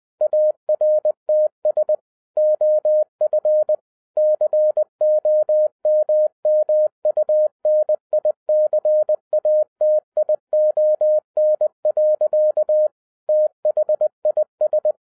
Morse code, heard as a keyed tone, about 20 wpm, 610 Hz. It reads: ARTS OF COMMUNICATION. THIS